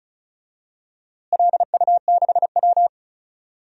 Morse code, heard as a keyed tone, about 36 wpm, 700 Hz. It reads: LU6W
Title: DXCC-100 Call Signs - 4 Characters - Repeated 3x - 35wpm